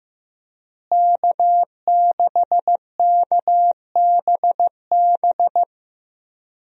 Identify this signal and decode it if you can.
Morse code, heard as a keyed tone, about 15 wpm, 700 Hz. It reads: K6KBB
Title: USA Call Signs - 5 Characters - Repeated 3x - 15wpm